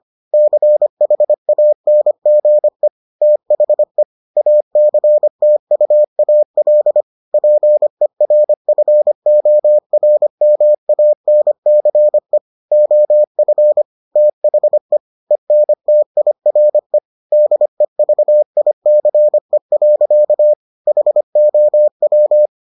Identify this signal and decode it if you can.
Morse code, heard as a keyed tone, about 25 wpm, 600 Hz. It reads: CHANGE THE ACTUAL PERFORMANCE OF THE ENTIRE DEVICE. HOW